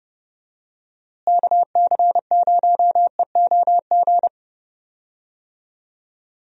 Morse code, heard as a keyed tone, about 31 wpm, 700 Hz. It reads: KC0EOG
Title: USA Call Signs - 6 Characters - Repeated 3x - 30wpm